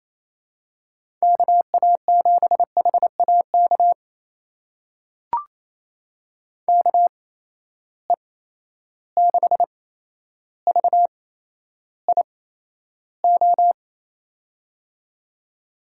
Morse code, heard as a keyed tone, about 28 wpm, spaced as for 12 wpm, 700 Hz. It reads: KA7HAK KE6VIO
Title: USA Call Signs - 6 Characters - 8x Character Spacing - 28wpm